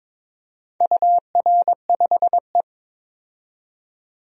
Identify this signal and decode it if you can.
Morse code, heard as a keyed tone, about 22 wpm, 700 Hz. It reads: UR5E
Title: DXCC-100 Call Signs - 4 Characters - 22wpm